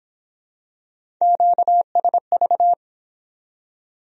Morse code, heard as a keyed tone, about 23 wpm, 700 Hz. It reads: QSV